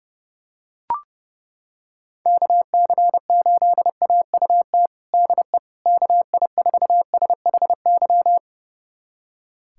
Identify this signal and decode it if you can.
Morse code, heard as a keyed tone, about 32 wpm, 700 Hz. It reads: KC8AUT DE KI4SHY
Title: USA Callsign DE Callsign - 6 Characters - 30wpm